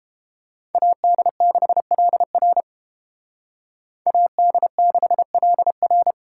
Morse code, heard as a keyed tone, about 28 wpm, 700 Hz. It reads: AD6LR AD6LR